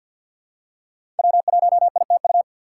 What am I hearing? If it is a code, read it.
Morse code, heard as a keyed tone, about 48 wpm, 700 Hz. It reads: W1ITU
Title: USA Call Signs - 2x Character Spacing - 50wpm